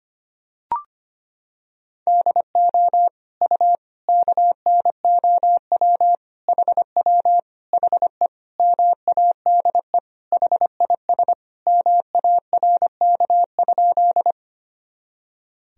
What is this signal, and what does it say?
Morse code, heard as a keyed tone, about 25 wpm, 700 Hz. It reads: DO U KNOW HW HE MADE HIS MARK?
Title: Sentences from Top 300 Words and Abbreviations 25wpm